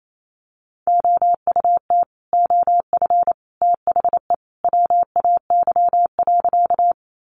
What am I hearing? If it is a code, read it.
Morse code, heard as a keyed tone, about 28 wpm, 700 Hz. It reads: OUT OF THE WAY.
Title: Sentences from Top 100 Words 28wpm